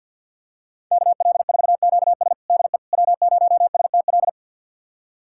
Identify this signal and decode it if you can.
Morse code, heard as a keyed tone, about 46 wpm, 700 Hz. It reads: KL4QS DE W0STL
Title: USA Callsign DE Callsign - 5 Characters - 50wpm